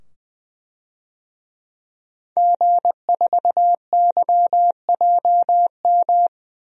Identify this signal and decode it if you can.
Morse code, heard as a keyed tone, about 21 wpm, 700 Hz. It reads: G4YJM
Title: DXCC-100 Call Signs - 5 Characters - 20wpm